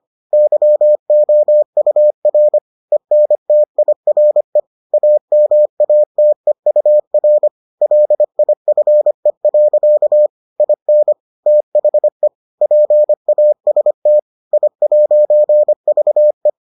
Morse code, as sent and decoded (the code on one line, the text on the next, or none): -.-- --- ..- .-. / . -. - .. .-. / .- -- .- - . ..- .-. / .-.. .. ..-. .-.-.- / .. -. / - .... / .--. .- ... - / .. .----. ...- .
YOUR ENTIRE AMATEUR LIFE. IN THE PAST I'VE